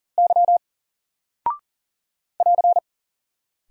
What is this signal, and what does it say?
Morse code, heard as a keyed tone, about 40 wpm, 700 Hz. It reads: Y <AR>
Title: Single Character (Letter:Number:Punctuation:Prosign) Alt 40wpm